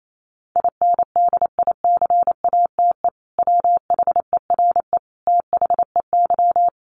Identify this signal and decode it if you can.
Morse code, heard as a keyed tone, about 28 wpm, 700 Hz. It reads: INDICATE WHERE THEY
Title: Sentences from Top 1300 Words 28wpm